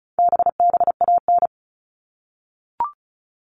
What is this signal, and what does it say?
Morse code, heard as a keyed tone, about 34 wpm, 700 Hz. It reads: BBAN
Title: All Cut Numbers - 4-Digit 35wpm